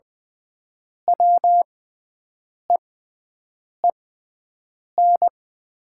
Morse code, heard as a keyed tone, about 20 wpm, spaced as for 7 wpm, 700 Hz. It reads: WEEN
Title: 6x Character Spacing - Top 300 Words - 20wpm